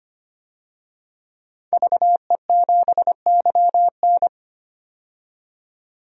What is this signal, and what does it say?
Morse code, heard as a keyed tone, about 25 wpm, 700 Hz. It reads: VE7YN
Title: DXCC-100 Call Signs - 5 Characters - 25wpm